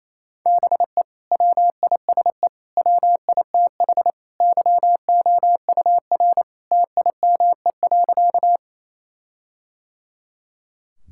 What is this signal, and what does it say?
Morse code, heard as a keyed tone, about 28 wpm, 700 Hz. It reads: BE WISE WITH YOUR TIME.
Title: Sentences from Top 1600 Words 28wpm